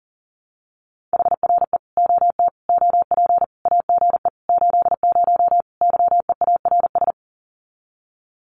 Morse code, heard as a keyed tone, about 40 wpm, 700 Hz. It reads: HRE OT OP AGE 80 YEARS
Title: Common QSO Phrases 40wpm